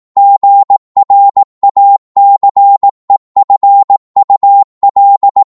Morse code, heard as a keyed tone, about 18 wpm, 800 Hz. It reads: GRACEFUL
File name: 18wpm